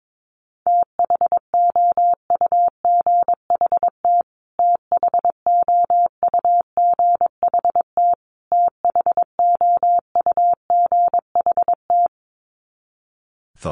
Morse code, heard as a keyed tone, about 22 wpm, 700 Hz. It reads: THOUGHT THOUGHT THOUGHT
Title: Top 500 Words Repeated 22wpm